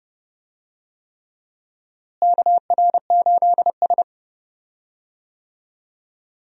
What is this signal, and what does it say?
Morse code, heard as a keyed tone, about 31 wpm, 700 Hz. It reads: KR8S
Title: USA Call Signs - 4 Characters - Repeated 3x - 30wpm